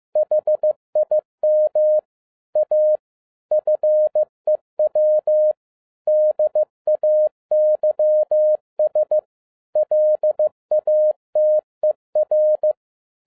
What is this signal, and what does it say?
Morse code, heard as a keyed tone, about 15 wpm, 610 Hz. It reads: HIM A FEW DAYS LATER
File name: 2011-10-mp3 / n07-A DOT 15wpm.mp3